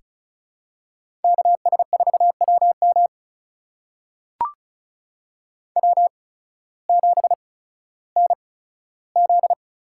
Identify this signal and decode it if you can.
Morse code, heard as a keyed tone, about 36 wpm, spaced as for 16 wpm, 700 Hz. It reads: KS4WM W7NZ
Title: USA Call Signs - 8x Character Spacing - 35wpm